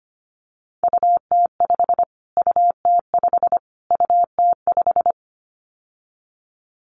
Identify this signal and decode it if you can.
Morse code, heard as a keyed tone, about 25 wpm, 700 Hz. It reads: UT5 UT5 UT5
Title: Callsign Prefixes for Top 25 DXCC - Repeated 3x - 25wpm